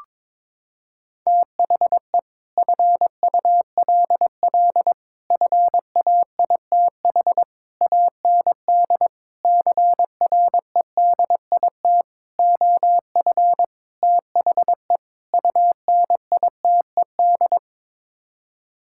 Morse code, heard as a keyed tone, about 22 wpm, 700 Hz. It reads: THE FULL FAITH AND CREDIT OF THE UNITED
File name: practice / 22wpm / Sets of 8 Words - 22wpm